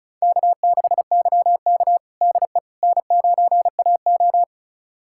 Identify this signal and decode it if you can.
Morse code, heard as a keyed tone, about 34 wpm, 690 Hz. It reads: K6YK DE N9AO
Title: USA Callsign DE Callsign - 4 Characters - 35wpm